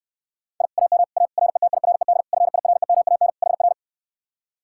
Morse code, heard as a keyed tone, about 69 wpm, 700 Hz. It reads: I AM A RETIRED FIREMAN HR